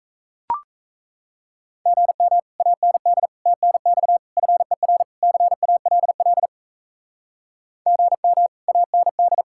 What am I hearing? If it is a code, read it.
Morse code, heard as a keyed tone, about 37 wpm, 700 Hz. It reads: GM AND TNX FER CALL GM AND